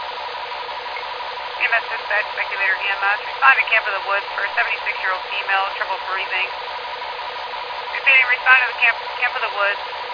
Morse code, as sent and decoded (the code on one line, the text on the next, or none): none